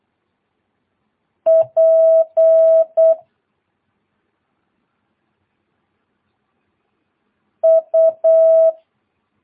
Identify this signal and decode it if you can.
Morse code, heard as a keyed tone, about 8 wpm, 650 Hz. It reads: P U
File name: jh8cbh / 2022cw